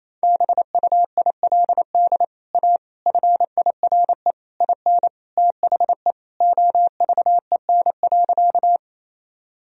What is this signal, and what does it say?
Morse code, heard as a keyed tone, about 28 wpm, 700 Hz. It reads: BUILD A FIRE IN THE OVEN.
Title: Sentences from Top 1000 Words - Review Part 4 - 28wpm